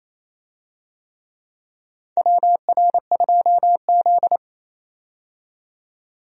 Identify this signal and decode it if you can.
Morse code, heard as a keyed tone, about 27 wpm, 700 Hz. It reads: WR2Z
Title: USA Call Signs - 4 Characters - 28wpm